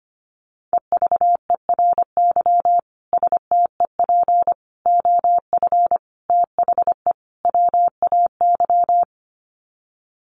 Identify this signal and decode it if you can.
Morse code, heard as a keyed tone, about 25 wpm, 700 Hz. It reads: EVERY STEP OF THE WAY
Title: Sets of 5 Words Encore2 25wpm